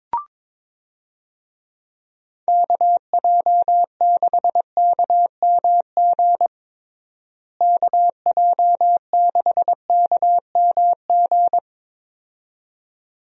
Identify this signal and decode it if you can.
Morse code, heard as a keyed tone, about 22 wpm, 700 Hz. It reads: KJ6KMG KJ6KMG
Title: USA Call Signs - 6 Characters - Repeated 2x - 22wpm